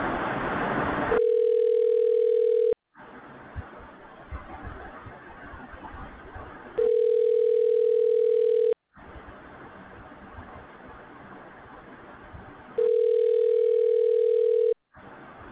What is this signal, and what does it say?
Ringback tone (North America), 440 + 480 Hz.
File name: recordings / REd e5c2b76e005aa19510d428d0f